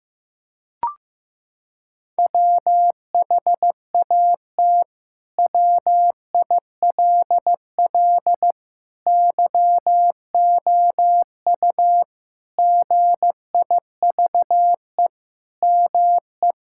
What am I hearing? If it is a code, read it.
Morse code, heard as a keyed tone, about 15 wpm, 700 Hz. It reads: WHAT WILL YOU GIVE ME
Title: Sentences from Top 1000 Words - Review Part 6 - 15wpm